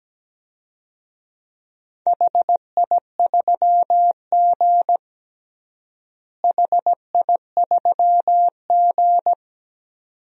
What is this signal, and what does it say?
Morse code, heard as a keyed tone, about 17 wpm, 700 Hz. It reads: HI3G HI3G
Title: DXCC-100 Call Signs - 4 Characters - Repeated 2x - 17wpm